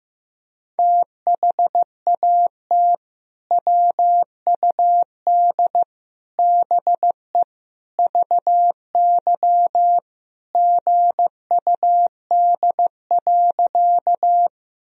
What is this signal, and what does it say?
Morse code, heard as a keyed tone, about 15 wpm, 700 Hz. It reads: THAT WUD BE VY GUD.
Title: Sentences from Top 200 Words and Abbreviations 15wpm